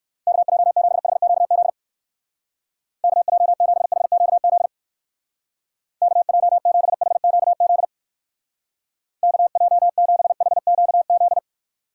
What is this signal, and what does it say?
Morse code, heard as a keyed tone, about 57 wpm, 700 Hz. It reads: KJ7HQZ KJ7HQZ KJ7HQZ KJ7HQZ